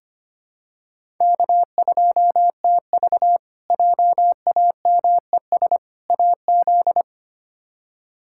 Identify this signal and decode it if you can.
Morse code, heard as a keyed tone, about 25 wpm, 700 Hz. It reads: K2TV JAMES AZ
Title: K1USN SST QSO Practice - 25wpm